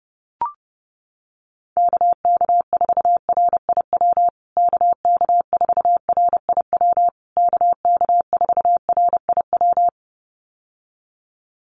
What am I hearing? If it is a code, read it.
Morse code, heard as a keyed tone, about 30 wpm, 700 Hz. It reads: KK4RIW KK4RIW KK4RIW
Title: USA Call Signs Repeated 3x 30wpm